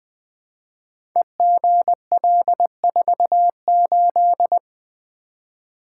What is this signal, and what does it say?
Morse code, heard as a keyed tone, about 20 wpm, 700 Hz. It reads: EGL48
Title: Code Groups - 5 Letter:Numbers - 20wpm